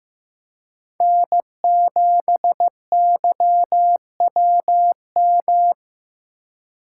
Morse code, heard as a keyed tone, about 16 wpm, 700 Hz. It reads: N7YWM
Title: USA Call Signs - 3x Character Spacing - 15wpm